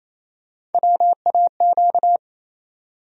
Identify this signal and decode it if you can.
Morse code, heard as a keyed tone, about 28 wpm, 700 Hz. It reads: WAQ
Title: USA Call Sign Postfixes - 3 Letters 28wpm